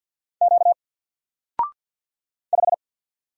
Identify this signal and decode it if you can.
Morse code, heard as a keyed tone, about 48 wpm, 700 Hz. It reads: Q 5
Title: Single Letter-Number 50wpm